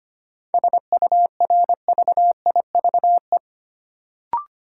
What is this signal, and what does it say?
Morse code, heard as a keyed tone, about 25 wpm, 700 Hz. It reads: SURVIVE
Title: Sentences from Top 1300 Words - New Word Drill Alt - 25wpm